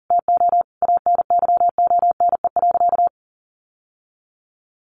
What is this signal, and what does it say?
Morse code, heard as a keyed tone, about 40 wpm, 700 Hz. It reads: TO ANYONE.
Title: Sentences from Top 1200 Words 40wpm